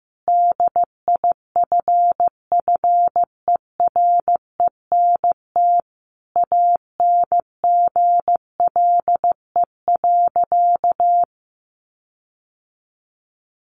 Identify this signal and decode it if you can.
Morse code, heard as a keyed tone, about 14 wpm, 700 Hz. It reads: DIFFERENT ANGLE.
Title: Sentences from Top 1500 Words 15wpm